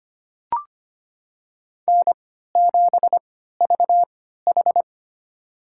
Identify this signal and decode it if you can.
Morse code, heard as a keyed tone, about 25 wpm, spaced as for 14 wpm, 700 Hz. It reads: N7VH
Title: USA Call Signs - 4 Characters - 3x Character Spacing - 25wpm